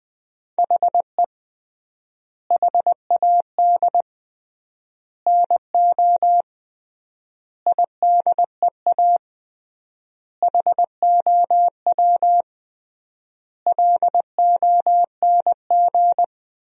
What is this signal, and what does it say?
Morse code, heard as a keyed tone, about 21 wpm, 700 Hz. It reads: HE HAD NO IDEA HOW LONG
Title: 3x Word-Spacing - Sentences from Top 300 Words - 20wpm